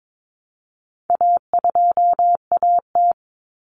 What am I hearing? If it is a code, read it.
Morse code, heard as a keyed tone, about 22 wpm, 700 Hz. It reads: A2AT
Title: Common Cut Numbers - 4-Digit 22wpm